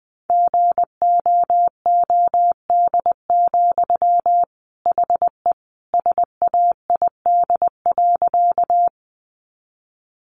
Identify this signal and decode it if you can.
Morse code, heard as a keyed tone, about 19 wpm, 700 Hz. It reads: GOOD, HE SAID.
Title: Sentences from Top 200 Words 20wpm